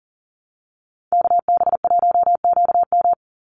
Morse code, heard as a keyed tone, about 40 wpm, 700 Hz. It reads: KB1QM
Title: USA Call Signs 40wpm